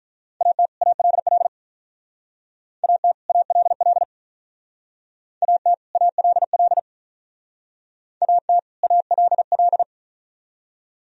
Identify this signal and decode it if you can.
Morse code, heard as a keyed tone, about 43 wpm, 700 Hz. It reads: AT ALL AT ALL AT ALL AT ALL
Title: Speed-Racing - Sets of 2 Words 35wpm